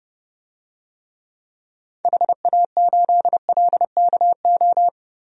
Morse code, heard as a keyed tone, about 30 wpm, 700 Hz. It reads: HA8LKO
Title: DXCC-100 Call Signs - 6 Characters - 30wpm